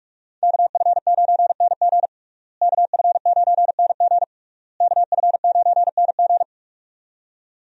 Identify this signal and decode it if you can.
Morse code, heard as a keyed tone, about 44 wpm, 700 Hz. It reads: KF9NG KF9NG KF9NG
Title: USA Call Signs - 5 Characters - Repeated 3x - 45wpm